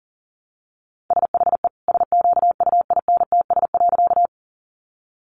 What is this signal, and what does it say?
Morse code, heard as a keyed tone, about 40 wpm, 700 Hz. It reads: SHE SQUINTS.
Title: Additional Easy Phrases 40wpm